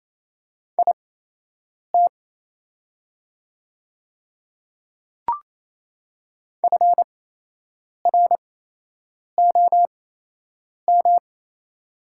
Morse code, heard as a keyed tone, about 28 wpm, spaced as for 8 wpm, 690 Hz. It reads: IT FROM
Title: 8x Character Spacing - Top 200 Words - 28wpm